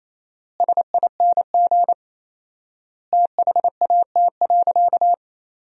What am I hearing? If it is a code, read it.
Morse code, heard as a keyed tone, about 28 wpm, 700 Hz. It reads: SING THAT.
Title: 4x Word-Spacing - Sentences from Top 500 Words - 28wpm